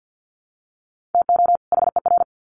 Morse code, heard as a keyed tone, about 50 wpm, 700 Hz. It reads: TO HER
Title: Sets of 2 Words 50wpm